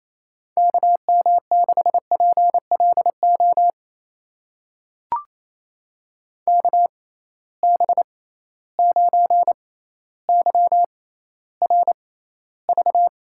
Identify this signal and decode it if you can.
Morse code, heard as a keyed tone, about 28 wpm, spaced as for 14 wpm, 700 Hz. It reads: KM6PLO KB9YRV
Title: USA Call Signs - 6 Characters - 6x Character Spacing - 28wpm